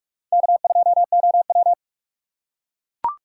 Morse code, heard as a keyed tone, about 40 wpm, 700 Hz. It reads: K2OW